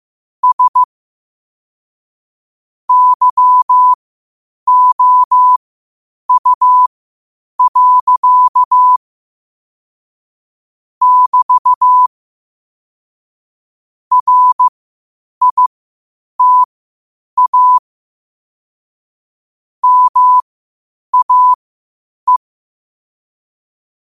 Morse code, heard as a keyed tone, about 15 wpm, spaced as for 8 wpm, 1000 Hz. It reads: S YOU. <BT> RITA MAE
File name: qotdfiles / 8wpm